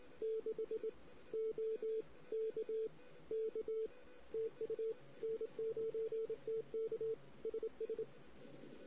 Morse code, heard as a keyed tone, about 20 wpm, 440 Hz. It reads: 6 O K K TU N9TK SS
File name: CW / 2008-ss-cw-k5tr-op-n5rz -16:00